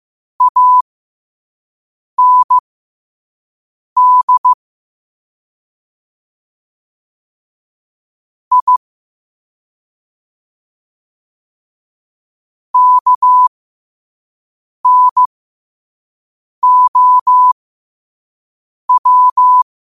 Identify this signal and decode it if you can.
Morse code, heard as a keyed tone, about 15 wpm, spaced as for 5 wpm, 1000 Hz. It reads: AND I KNOW
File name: qotdfiles / 5wpm